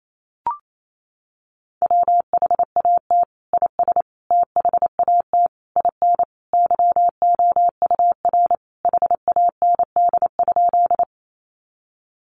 Morse code, heard as a keyed tone, about 28 wpm, 700 Hz. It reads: WHAT IS THAT IN YOUR HAND?
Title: Sentences from Top 200 Words 28wpm